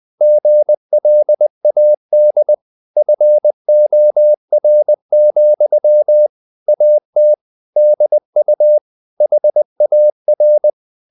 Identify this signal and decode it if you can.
Morse code, heard as a keyed tone, about 20 wpm, 600 Hz. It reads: GLAD FOR, AT DU HAR
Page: Ferie 20 wpm